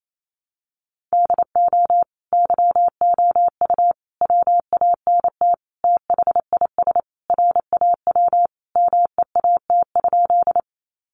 Morse code, heard as a keyed tone, about 28 wpm, 700 Hz. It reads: DO YOU WANT THIS RAW MEAT?
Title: Sentences from Top 1500 Words 28wpm